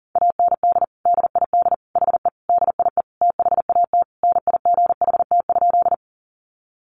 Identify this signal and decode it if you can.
Morse code, heard as a keyed tone, about 40 wpm, 700 Hz. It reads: AND DID HE DIE THAT NIGHT?
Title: Sentences from Top 1300 Words 40wpm